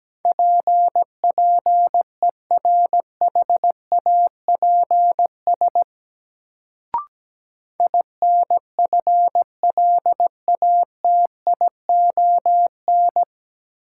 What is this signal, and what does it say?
Morse code, heard as a keyed tone, about 17 wpm, 700 Hz. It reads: PPERHAPS INFLATION
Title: Sentences from Top 1100 Words - New Word Drill - 17wpm